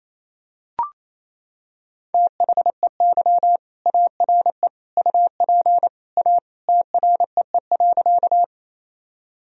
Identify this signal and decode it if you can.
Morse code, heard as a keyed tone, about 28 wpm, 700 Hz. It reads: THEY ARE UP A TREE.
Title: Sentences from Top 1000 Words - Review Part 1 - 28wpm